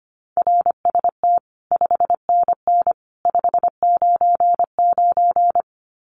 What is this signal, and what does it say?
Morse code, heard as a keyed tone, about 25 wpm, 700 Hz. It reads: RST 5NN 599
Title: Common QSO Phrases 25wpm